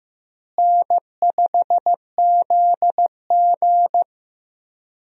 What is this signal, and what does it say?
Morse code, heard as a keyed tone, about 15 wpm, 700 Hz. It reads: N5ZG
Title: USA Call Signs - 4 Characters - Repeated 2x - 15wpm